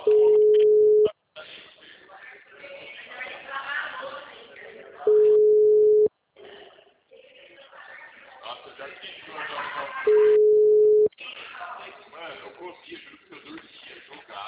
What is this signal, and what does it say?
Ringback tone (Europe, 425 Hz), with speech.